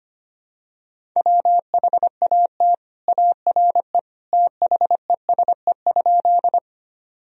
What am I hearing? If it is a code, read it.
Morse code, heard as a keyed tone, about 25 wpm, 700 Hz. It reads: WHAT ARE THESE?
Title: Sentences from Top 1000 Words - Review Part 1 - 25wpm